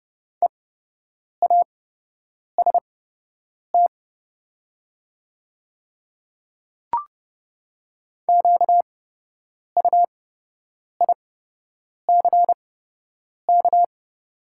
Morse code, heard as a keyed tone, about 30 wpm, spaced as for 8 wpm, 700 Hz. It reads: EAST QUICK
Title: 8x Character Spacing - Top 500 Words - 30wpm